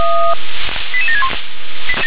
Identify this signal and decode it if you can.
Five-tone selective call (ZVEI), heard as quick runs of five tones, tones of about 70 ms. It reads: 88761
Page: /357/SDR/ZVEI/